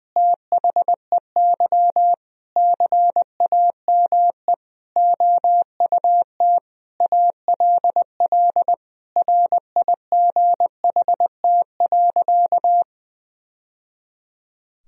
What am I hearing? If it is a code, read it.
Morse code, heard as a keyed tone, about 20 wpm, 700 Hz. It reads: THEY CAME OUT ALL RIGHT.
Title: Sentences from Top 1000 Words - Review Part 4 - 20wpm